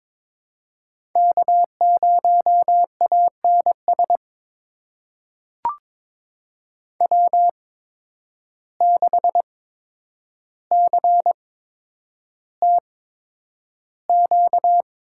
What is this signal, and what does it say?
Morse code, heard as a keyed tone, about 22 wpm, spaced as for 9 wpm, 700 Hz. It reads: K0ANS W6CTQ
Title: USA Call Signs - 5 Characters - 8x Character Spacing - 22wpm